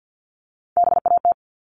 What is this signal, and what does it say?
Morse code, heard as a keyed tone, about 50 wpm, 700 Hz. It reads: BAT
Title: Top 1251-1500 Words 50wpm